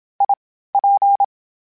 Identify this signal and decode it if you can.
Morse code, heard as a keyed tone, about 27 wpm, 800 Hz. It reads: I P
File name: Variable Speed / Variable Speed-mp3s / T AOEI's sp @25